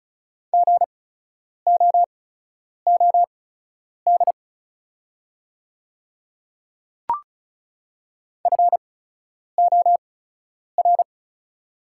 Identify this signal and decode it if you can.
Morse code, heard as a keyed tone, about 36 wpm, spaced as for 10 wpm, 690 Hz. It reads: GOOD FOR
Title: 8x Character Spacing - Top 200 Words - 35wpm